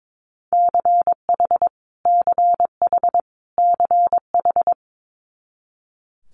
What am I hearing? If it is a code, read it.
Morse code, heard as a keyed tone, about 22 wpm, 700 Hz. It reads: CH CH CH
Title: Multi-letter Phonemes Repeated 3x 22wpm